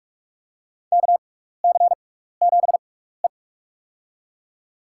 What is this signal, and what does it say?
Morse code, heard as a keyed tone, about 46 wpm, spaced as for 16 wpm, 700 Hz. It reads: KC7E
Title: USA Call Signs - 4 Characters - 6x Character Spacing - 45wpm